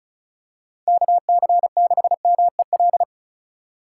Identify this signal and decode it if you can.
Morse code, heard as a keyed tone, about 36 wpm, 700 Hz. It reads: KC6MEL
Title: USA Call Signs - 6 Characters - 4x Character Spacing - 35wpm